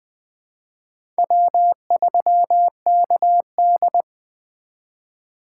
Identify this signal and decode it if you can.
Morse code, heard as a keyed tone, about 20 wpm, 700 Hz. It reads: W3KD
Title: USA Call Signs - 4 Characters - 3x Character Spacing - 20wpm